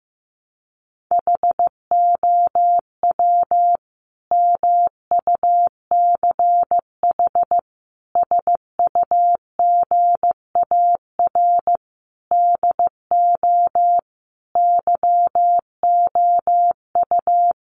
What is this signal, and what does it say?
Morse code, heard as a keyed tone, about 15 wpm, 700 Hz. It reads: HOW MUCH SUGAR DO YOU
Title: Sentences from Top 1200 Words 15wpm